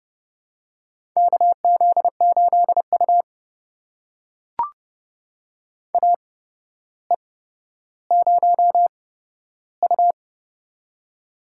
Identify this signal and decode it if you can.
Morse code, heard as a keyed tone, about 30 wpm, spaced as for 13 wpm, 700 Hz. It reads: KZ8U AE0U
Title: USA Call Signs - 4 Characters - 8x Character Spacing - 30wpm